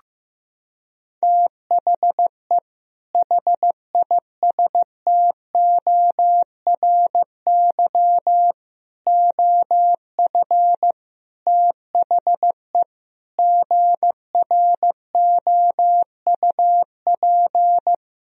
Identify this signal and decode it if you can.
Morse code, heard as a keyed tone, about 15 wpm, 700 Hz. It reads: THE HISTORY OF THE GROUP
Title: Sentences from Top 1000 Words - Review Part 1 - 15wpm